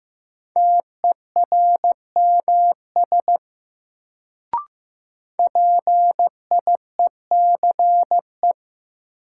Morse code, heard as a keyed tone, about 15 wpm, 700 Hz. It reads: TERMS PIECE
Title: Sets of 3 Words - New Word Drill - 15wpm